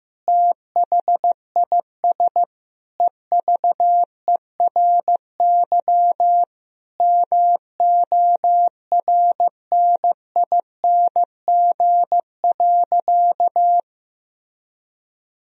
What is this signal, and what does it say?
Morse code, heard as a keyed tone, about 15 wpm, 700 Hz. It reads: THIS EVERY MORNING.
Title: Sentences from Top 1000 Words - Review Part 3 - 15wpm